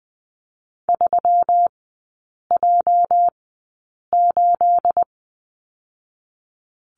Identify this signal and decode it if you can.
Morse code, heard as a keyed tone, about 20 wpm, spaced as for 8 wpm, 690 Hz. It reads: 3J8
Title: 212 Lesson 34e - Number 8 - Three Characters 20wpm